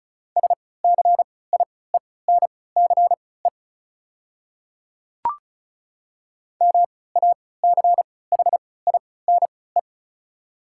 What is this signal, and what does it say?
Morse code, heard as a keyed tone, about 36 wpm, spaced as for 20 wpm, 700 Hz. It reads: SCIENCE MACHINE
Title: 3x Character Spacing - Top 500 Words - 35wpm